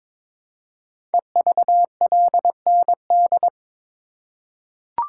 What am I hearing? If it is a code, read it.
Morse code, heard as a keyed tone, about 22 wpm, 700 Hz. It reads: EVLND
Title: Code Groups - 5 Letter:Numbers - 22wpm